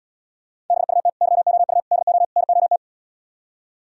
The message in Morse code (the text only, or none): BUT YOU DO NOT